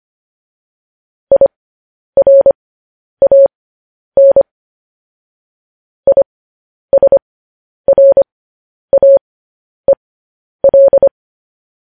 Morse code, heard as a keyed tone, about 25 wpm, spaced as for 10 wpm, 550 Hz. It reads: IRAN ISRAEL